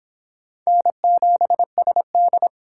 Morse code, heard as a keyed tone, about 26 wpm, 700 Hz. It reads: N7SD